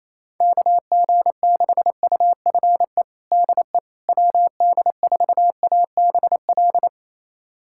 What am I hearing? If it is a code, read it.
Morse code, heard as a keyed tone, about 28 wpm, 700 Hz. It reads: KG6UFE DE WD4ABL
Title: USA Callsign DE Callsign 28wpm